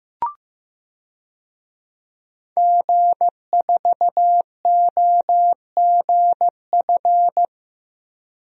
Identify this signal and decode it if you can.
Morse code, heard as a keyed tone, about 16 wpm, 700 Hz. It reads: G4OGF
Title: DXCC-100 Call Signs - 5 Characters - Repeated 2x - 15wpm